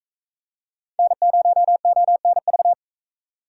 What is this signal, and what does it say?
Morse code, heard as a keyed tone, about 44 wpm, 700 Hz. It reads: N0ONV